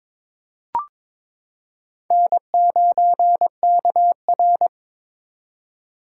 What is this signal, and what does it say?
Morse code, heard as a keyed tone, about 22 wpm, 700 Hz. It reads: N9KR
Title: K1USN SST QSO Practice - 22wpm